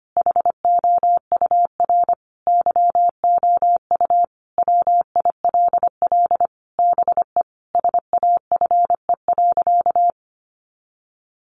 Morse code, heard as a keyed tone, about 25 wpm, 700 Hz. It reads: HOUR YOU WILL BE SAFE.
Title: Sentences from Top 1300 Words 25wpm